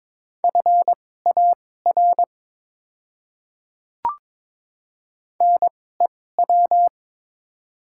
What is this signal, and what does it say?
Morse code, heard as a keyed tone, about 22 wpm, spaced as for 16 wpm, 700 Hz. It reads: FAR NEW
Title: 2x Character Spacing - Top 300 Words - 22wpm